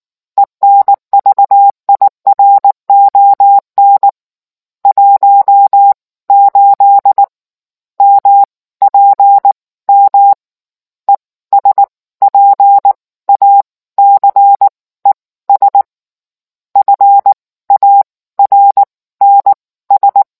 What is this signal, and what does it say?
Morse code, heard as a keyed tone, about 19 wpm, spaced as for 15 wpm, 800 Hz. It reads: ENVIRON 18 MPM ESPACES FARNS